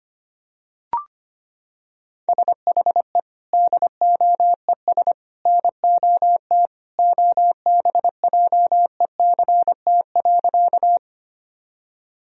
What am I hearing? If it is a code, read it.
Morse code, heard as a keyed tone, about 25 wpm, 700 Hz. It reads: SHE DOES NOT OBJECT.
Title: Sentences from Top 1000 Words - Review Part 3 - 25wpm